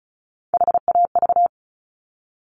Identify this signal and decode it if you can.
Morse code, heard as a keyed tone, about 36 wpm, 700 Hz. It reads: HAV
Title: Top Three Letter Combinations in Words 35wpm